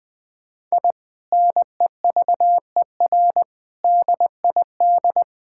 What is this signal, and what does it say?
Morse code, heard as a keyed tone, about 20 wpm, 700 Hz. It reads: I NEVER DID
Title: Sentences from Top 1000 Words - Review Part 5 - 20wpm